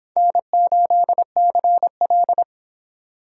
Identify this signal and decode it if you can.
Morse code, heard as a keyed tone, about 23 wpm, 700 Hz. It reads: N8CL